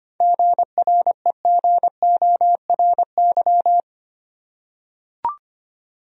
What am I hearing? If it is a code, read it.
Morse code, heard as a keyed tone, about 25 wpm, 700 Hz. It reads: GREGORY
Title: USA Common Names - 25wpm